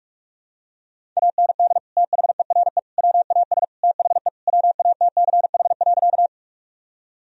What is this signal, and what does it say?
Morse code, heard as a keyed tone, about 41 wpm, 700 Hz. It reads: AND THERE WAS THE WATCH.
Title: Sentences from Top 1000 Words - Review Part 1 - 45wpm